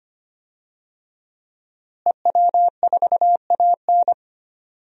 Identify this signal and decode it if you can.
Morse code, heard as a keyed tone, about 25 wpm, 700 Hz. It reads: EW4AN
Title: DXCC-100 Call Signs - 5 Characters - Repeated 3x - 25wpm